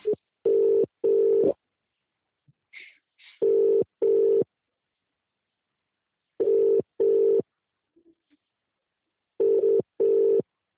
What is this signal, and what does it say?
Ringback tone (UK), 400 + 450 Hz.